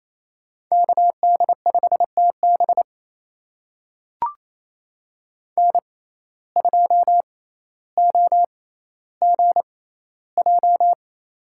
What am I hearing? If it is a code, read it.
Morse code, heard as a keyed tone, about 28 wpm, spaced as for 14 wpm, 700 Hz. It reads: KD5TB N2OGJ
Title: USA Call Signs - 5 Characters - 6x Character Spacing - 28wpm